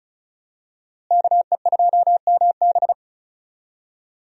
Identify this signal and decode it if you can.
Morse code, heard as a keyed tone, about 36 wpm, 700 Hz. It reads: KE2MB
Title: USA Call Signs - 5 Characters - Repeated 2x - 35wpm